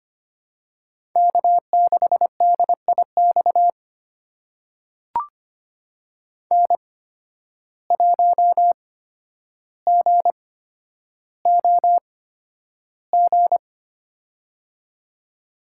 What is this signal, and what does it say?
Morse code, heard as a keyed tone, about 25 wpm, spaced as for 11 wpm, 700 Hz. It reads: K6DIX N1GOG
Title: USA Call Signs - 5 Characters - 8x Character Spacing - 25wpm